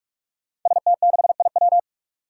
Morse code, heard as a keyed tone, about 46 wpm, 700 Hz. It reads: ST6IW
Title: Five Challenging Letter-Number Combinations 45wpm